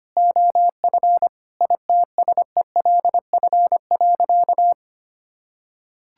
Morse code, heard as a keyed tone, about 25 wpm, 700 Hz. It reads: OF ITSELF.
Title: Sentences from Top 1000 Words - Review Part 5 - 25wpm